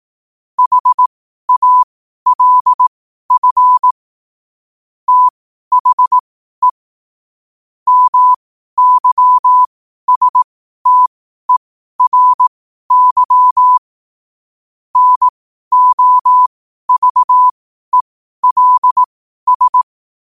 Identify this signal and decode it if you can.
Morse code, heard as a keyed tone, about 18 wpm, spaced as for 12 wpm, 1000 Hz. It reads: HALF THE MYSTERY NOVELS
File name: qotdfiles / 12wpm